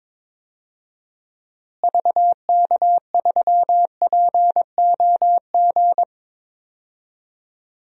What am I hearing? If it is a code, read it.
Morse code, heard as a keyed tone, about 22 wpm, 700 Hz. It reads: VK3POG
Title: DXCC-100 Call Signs - 6 Characters - Repeated 2x - 22wpm